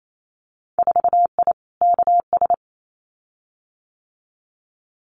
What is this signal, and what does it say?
Morse code, heard as a keyed tone, about 28 wpm, 700 Hz. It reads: 4I KS
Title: Winter Field Day Exchanges Encore 28wpm